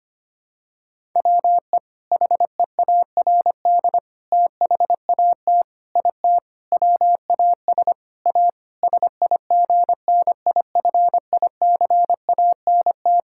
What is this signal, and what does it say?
Morse code, heard as a keyed tone, about 25 wpm, 700 Hz. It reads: WE HEARD THAT IT WAS A SIGNIFICANT
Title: Sentences from Top 1000 Words - Review Part 6 - 25wpm